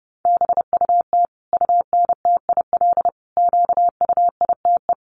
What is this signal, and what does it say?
Morse code, heard as a keyed tone, about 30 wpm, 700 Hz. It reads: BUT UNTIL QUITE
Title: Sentences from Top 1200 Words 30wpm